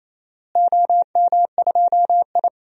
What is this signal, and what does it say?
Morse code, heard as a keyed tone, about 28 wpm, 700 Hz. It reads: OM2I
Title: DXCC-100 Call Signs - 4 Characters - Repeated 3x - 28wpm